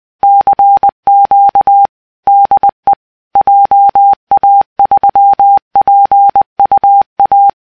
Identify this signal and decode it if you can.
Morse code, heard as a keyed tone, about 20 wpm, 800 Hz. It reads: CQ DE JA3PUA
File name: CQ DE JA3PUA K1